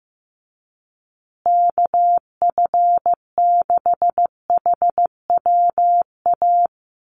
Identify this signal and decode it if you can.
Morse code, heard as a keyed tone, about 15 wpm, 700 Hz. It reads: KF6HWA
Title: USA Call Signs Repeated 2x 15wpm